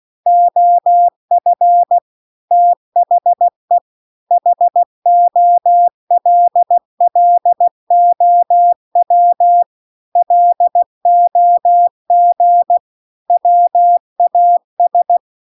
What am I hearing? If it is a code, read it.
Morse code, heard as a keyed tone, about 16 wpm, 700 Hz. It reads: OF THE HOLLOW LOG WAS